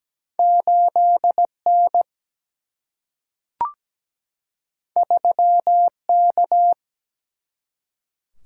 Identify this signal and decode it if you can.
Morse code, heard as a keyed tone, about 17 wpm, 700 Hz. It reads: 8 N 3 K
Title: Single-Letter Single-Number 17wpm